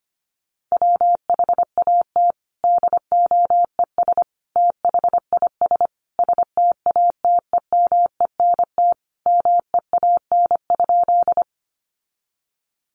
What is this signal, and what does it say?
Morse code, heard as a keyed tone, about 25 wpm, 700 Hz. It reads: WHAT DOES THIS STATEMENT MEAN?
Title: Sentences from Top 1200 Words 25wpm